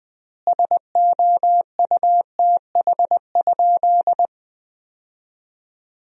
Morse code, heard as a keyed tone, about 21 wpm, 700 Hz. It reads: SOUTH?
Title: 4x Word-Spacing - Sentences from Top 400 Words - 20wpm